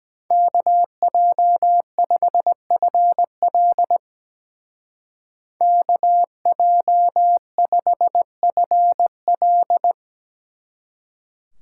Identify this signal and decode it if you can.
Morse code, heard as a keyed tone, about 19 wpm, 700 Hz. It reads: KJ5FL KJ5FL
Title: Speed Racing - USA 2x2 Call Signs 17wpm